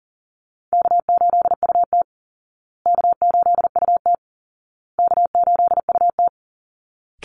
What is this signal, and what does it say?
Morse code, heard as a keyed tone, about 40 wpm, 700 Hz. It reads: K8UT K8UT K8UT
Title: Easy Call Signs Repeated 3x - 40wpm